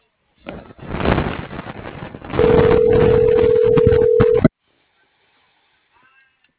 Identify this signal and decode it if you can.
Ringback tone (North America), 440 + 480 Hz.
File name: CENTRAL TERRESTRE / out claudiacruz - .62699